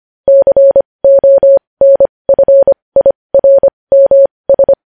Morse code, heard as a keyed tone, about 25 wpm, spaced as for 20 wpm, 550 Hz. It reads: CONFIRMS
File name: podcast / News headlines-1x-25x20